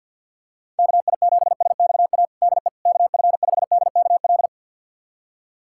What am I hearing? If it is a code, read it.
Morse code, heard as a keyed tone, about 53 wpm, 700 Hz. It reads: KI7SXA DE KF5DKL
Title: USA Callsign DE Callsign - 6 Characters - 50wpm